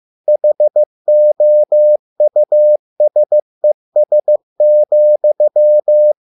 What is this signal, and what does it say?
Morse code, heard as a keyed tone, about 15 wpm, 600 Hz. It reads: HOUSES,